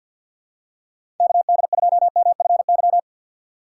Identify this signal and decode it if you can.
Morse code, heard as a keyed tone, about 52 wpm, 700 Hz. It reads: KD2MFY
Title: USA Call Signs - 6 Characters - 3x Character Spacing - 50wpm